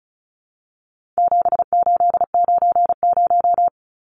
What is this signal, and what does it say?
Morse code, heard as a keyed tone, about 34 wpm, 700 Hz. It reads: 7890
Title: Numbers - 4-Digit 35wpm